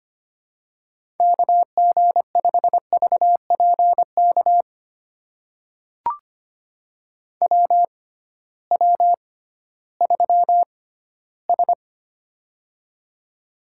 Morse code, heard as a keyed tone, about 25 wpm, spaced as for 15 wpm, 700 Hz. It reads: KG5VPK WW3S
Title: USA Call Signs - 6x Character Spacing - 25wpm